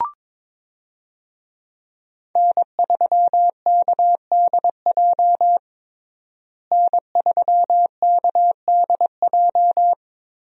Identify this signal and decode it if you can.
Morse code, heard as a keyed tone, about 21 wpm, 700 Hz. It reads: N3KDJ N3KDJ
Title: USA Call Signs - 5 Characters - Repeated 2x - 22wpm